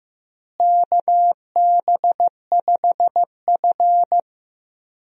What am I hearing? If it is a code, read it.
Morse code, heard as a keyed tone, about 15 wpm, 700 Hz. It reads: KB5F
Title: USA Call Signs - 4 Characters - Repeated 3x - 15wpm